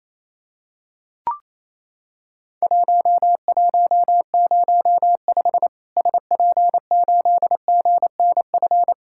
Morse code, heard as a keyed tone, about 28 wpm, 700 Hz. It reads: 1105 SP8GNF
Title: Worked All Europe DX Contest - QTC Exchange - 28wpm